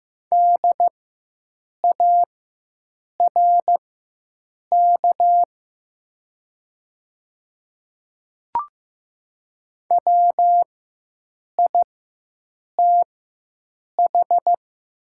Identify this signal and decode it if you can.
Morse code, heard as a keyed tone, about 15 wpm, spaced as for 7 wpm, 690 Hz. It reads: DARK WITH
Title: 4x Character Spacing - Top 500 Words - 15wpm